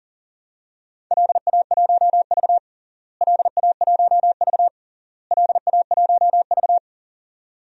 Morse code, heard as a keyed tone, about 40 wpm, 700 Hz. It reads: LA1V LA1V LA1V
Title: DXCC-100 Call Signs - 4 Characters - Repeated 3x - 40wpm